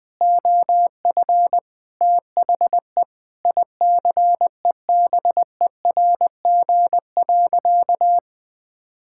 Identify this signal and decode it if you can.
Morse code, heard as a keyed tone, about 20 wpm, 700 Hz. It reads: OF THE ICEBERG.
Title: Sentences with 7-Word Phrases - 20wpm